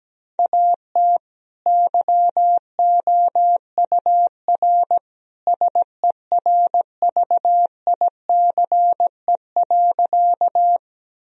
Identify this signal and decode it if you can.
Morse code, heard as a keyed tone, about 17 wpm, 700 Hz. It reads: AT YOUR SERVICE.
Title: Sentences from Top 1000 Words - Review Part 4 - 17wpm